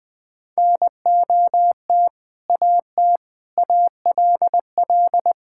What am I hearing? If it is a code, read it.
Morse code, heard as a keyed tone, about 21 wpm, 700 Hz. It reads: NOT AT ALL
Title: Sentences with 2-Word Phrases - 20wpm